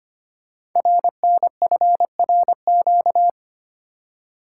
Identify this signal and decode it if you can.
Morse code, heard as a keyed tone, about 25 wpm, 700 Hz. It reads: RNFRQ
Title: Code Groups - 5 Letters - 25wpm